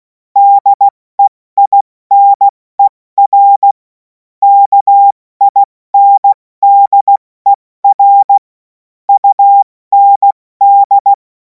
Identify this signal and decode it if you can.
Morse code, heard as a keyed tone, about 16 wpm, 800 Hz. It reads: DEINER KINDER UND
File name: Morsekurs / Witze / Witz 30. Juni 2026 70 F80